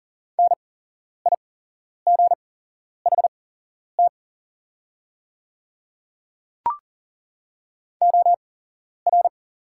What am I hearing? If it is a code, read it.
Morse code, heard as a keyed tone, about 40 wpm, spaced as for 11 wpm, 690 Hz. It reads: NIGHT OR
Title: 8x Character Spacing - Top 500 Words - 40wpm